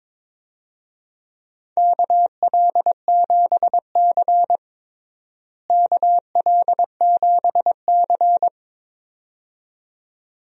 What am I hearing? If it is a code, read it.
Morse code, heard as a keyed tone, about 21 wpm, 700 Hz. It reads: KL7C KL7C
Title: DXCC-100 Call Signs - 4 Characters - Repeated 2x - 22wpm